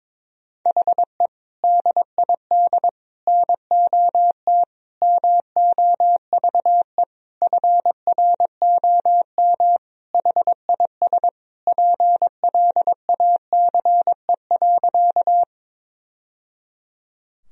Morse code, heard as a keyed tone, about 21 wpm, 700 Hz. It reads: HE DID NOT MOVE FROM HIS PLACE.
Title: Sentences from Top 1000 Words - Review Part 1 - 22wpm